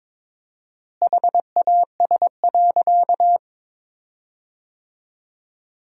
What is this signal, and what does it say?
Morse code, heard as a keyed tone, about 21 wpm, 700 Hz. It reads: HAS.
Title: 4x Word-Spacing - Sentences from Top 500 Words - 22wpm